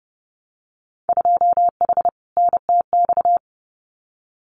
Morse code, heard as a keyed tone, about 30 wpm, 700 Hz. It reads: 2H NTX
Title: Winter Field Day Exchanges Encore 30wpm